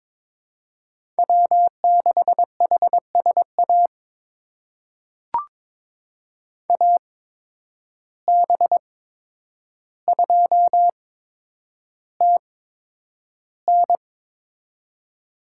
Morse code, heard as a keyed tone, about 22 wpm, spaced as for 9 wpm, 700 Hz. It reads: W6HSA AB2TN
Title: USA Call Signs - 8x Character Spacing - 22wpm